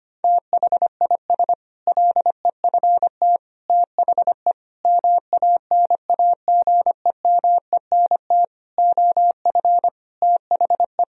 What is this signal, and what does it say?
Morse code, heard as a keyed tone, about 25 wpm, 700 Hz. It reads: THIS LEFT THE MANAGEMENT OF THE
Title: Sentences from Top 1000 Words - Review Part 4 - 25wpm